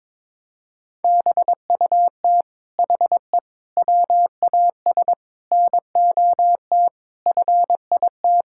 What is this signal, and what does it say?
Morse code, heard as a keyed tone, about 22 wpm, 700 Hz. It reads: BUT HE WAS NOT FIT
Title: Sentences from Top 1000 Words - Review Part 6 - 22wpm